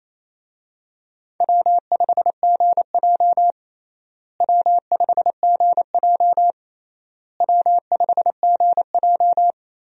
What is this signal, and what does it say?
Morse code, heard as a keyed tone, about 27 wpm, 700 Hz. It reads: W5GJ W5GJ W5GJ
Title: USA Call Signs - 4 Characters - Repeated 3x - 28wpm